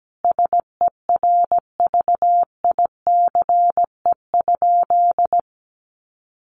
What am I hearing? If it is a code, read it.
Morse code, heard as a keyed tone, about 17 wpm, 700 Hz. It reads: SERVICE?
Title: Sentences from Top 1500 Words 17wpm